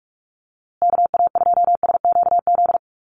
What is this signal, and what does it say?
Morse code, heard as a keyed tone, about 44 wpm, 700 Hz. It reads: KA2SQZ
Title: USA Call Signs Repeated 2x 45wpm